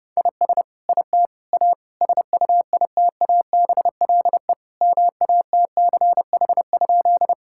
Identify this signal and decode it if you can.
Morse code, heard as a keyed tone, about 30 wpm, 700 Hz. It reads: IS IT A SUITABLE MATCH?
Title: Sentences from Top 1600 Words 30wpm